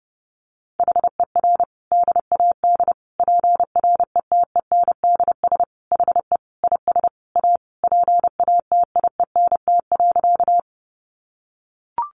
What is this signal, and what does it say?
Morse code, heard as a keyed tone, about 28 wpm, 700 Hz. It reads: HER DAD PRETENDS HE IS A PATIENT.
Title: Additional Easy Phrases 30wpm